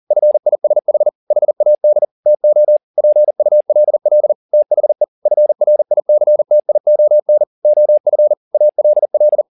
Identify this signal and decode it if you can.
Morse code, heard as a keyed tone, about 40 wpm, 600 Hz. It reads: FISH HAD TO PULL THE FRICTION OF ALL
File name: the old man and the sea 2 40wpm